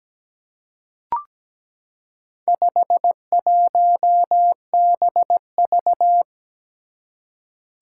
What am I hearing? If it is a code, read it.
Morse code, heard as a keyed tone, about 17 wpm, 700 Hz. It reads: 51BV
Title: Four Challenging Letter-Number Combinations 17wpm